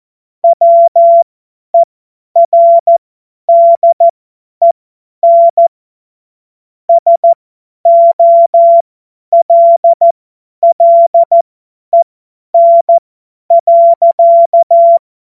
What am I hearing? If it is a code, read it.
Morse code, heard as a keyed tone, about 14 wpm, spaced as for 10 wpm, 680 Hz. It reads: WERDEN SOLLEN.
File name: Morsekurs / Witze / Witz 23 Januar 2024 50 F70